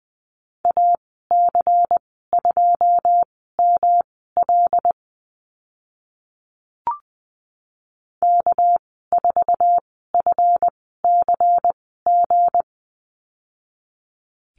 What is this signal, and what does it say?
Morse code, heard as a keyed tone, about 20 wpm, spaced as for 14 wpm, 700 Hz. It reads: AC2ML K4FCG
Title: 213 Lesson 34g - Number 8 - Call Signs 20wpm